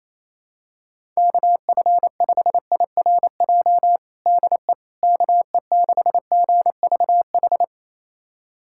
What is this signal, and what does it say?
Morse code, heard as a keyed tone, about 28 wpm, 700 Hz. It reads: KF5IRJ DE KE6GVH
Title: USA Callsign DE Callsign - 6 Characters - 28wpm